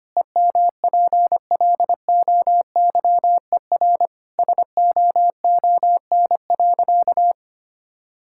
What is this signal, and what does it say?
Morse code, heard as a keyed tone, about 25 wpm, 700 Hz. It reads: EMPLOYER SOON.
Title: Sentences from Top 1600 Words 25wpm